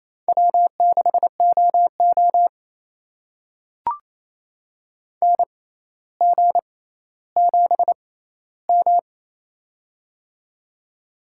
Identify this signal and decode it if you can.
Morse code, heard as a keyed tone, about 28 wpm, spaced as for 14 wpm, 700 Hz. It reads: W6OO NG7M
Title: USA Call Signs - 4 Characters - 6x Character Spacing - 28wpm